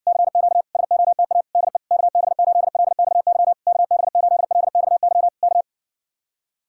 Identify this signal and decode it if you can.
Morse code, heard as a keyed tone, about 60 wpm, 700 Hz. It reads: CQ SOTA DE KB8LXY KB8LXY K